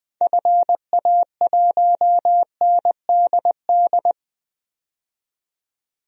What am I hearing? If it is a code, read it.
Morse code, heard as a keyed tone, about 20 wpm, 700 Hz. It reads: FA1NDD
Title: DXCC-100 Call Signs - 6 Characters - 20wpm